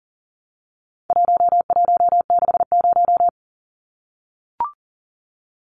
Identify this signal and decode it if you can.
Morse code, heard as a keyed tone, about 40 wpm, 700 Hz. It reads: 1160
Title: Numbers - 4-Digit 40wpm